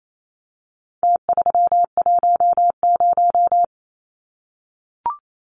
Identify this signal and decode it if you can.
Morse code, heard as a keyed tone, about 28 wpm, 700 Hz. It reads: T310
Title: Numbers - Contest Style - 4-Digit 28wpm